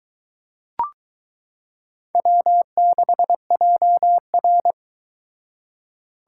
Morse code, heard as a keyed tone, about 22 wpm, 700 Hz. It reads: W6JR